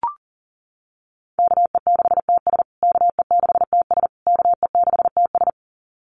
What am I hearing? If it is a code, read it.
Morse code, heard as a keyed tone, about 40 wpm, 700 Hz. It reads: KE6TS KE6TS KE6TS
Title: USA Call Signs Repeated 3x 40wpm